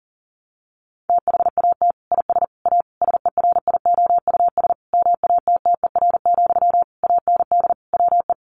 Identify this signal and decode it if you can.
Morse code, heard as a keyed tone, about 40 wpm, 700 Hz. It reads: THAT IS A SERIOUS MATTER, AND WE
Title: Sentences from Top 1100 Words 40wpm